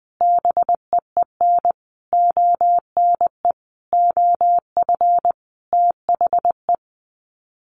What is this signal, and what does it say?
Morse code, heard as a keyed tone, about 20 wpm, 700 Hz. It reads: BEEN ONE OF THE
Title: Sets of 5 Words Encore 20wpm